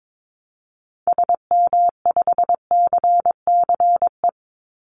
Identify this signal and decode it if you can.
Morse code, heard as a keyed tone, about 22 wpm, 700 Hz. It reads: SM5CCE
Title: Call Signs Repeated 3x - 22wpm